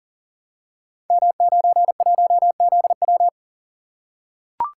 Morse code, heard as a keyed tone, about 40 wpm, 690 Hz. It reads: M91ZW
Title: Code Groups - 5 Letter:Numbers - 40wpm